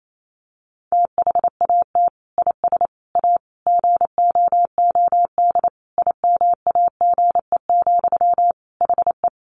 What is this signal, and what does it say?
Morse code, heard as a keyed tone, about 28 wpm, 700 Hz. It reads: THAT IS A GOOD IMAGE, HE
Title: Sentences from Top 700 Words 28wpm